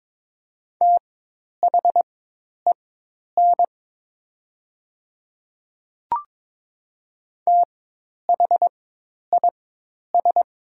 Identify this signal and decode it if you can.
Morse code, heard as a keyed tone, about 22 wpm, spaced as for 10 wpm, 700 Hz. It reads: THEN THIS
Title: 4x Character Spacing - Top 200 Words - 22wpm